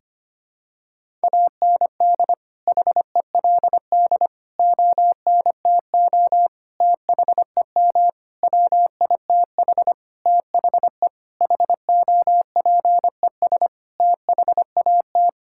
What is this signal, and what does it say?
Morse code, heard as a keyed tone, about 25 wpm, 700 Hz. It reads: AND HELD ONTO THEM WITH THE HOPES THAT
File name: practice / 25wpm / Sentences with 9-Word Phrases - 25wpm